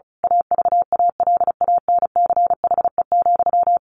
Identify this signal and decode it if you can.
Morse code, heard as a keyed tone, about 34 wpm, 700 Hz. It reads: AVALANCHE,
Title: A Sound Of Thunder 35wpm